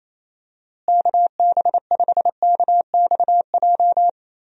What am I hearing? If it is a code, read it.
Morse code, heard as a keyed tone, about 28 wpm, 700 Hz. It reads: KB5KXJ
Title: USA Call Signs - 6 Characters - Repeated 3x - 28wpm